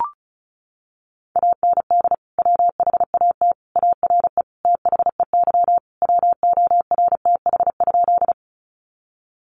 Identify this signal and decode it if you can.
Morse code, heard as a keyed tone, about 34 wpm, 700 Hz. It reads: AND WHAT ARE THEY WORTH?
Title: Sentences from Top 800 Words 35wpm